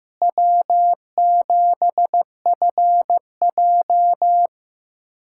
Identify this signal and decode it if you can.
Morse code, heard as a keyed tone, about 16 wpm, 700 Hz. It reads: W7FJ
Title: USA Callsign DE Callsign - 4 Characters - 15wpm